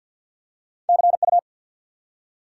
Morse code, heard as a keyed tone, about 51 wpm, 700 Hz. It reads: CU
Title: Top 90 QSO Elements 50wpm